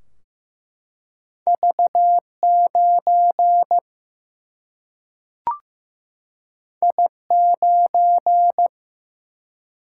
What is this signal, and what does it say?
Morse code, heard as a keyed tone, about 15 wpm, 700 Hz. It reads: V 9 I 9
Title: Single-Letter Single-Number 15wpm